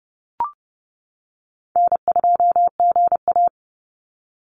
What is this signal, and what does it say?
Morse code, heard as a keyed tone, about 30 wpm, 700 Hz. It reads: N2GA
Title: Call Signs - 30wpm